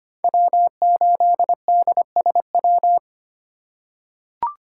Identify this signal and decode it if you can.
Morse code, heard as a keyed tone, about 25 wpm, 700 Hz. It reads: W8DSW
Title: USA Call Signs - 5 Characters - 4x Character Spacing - 25wpm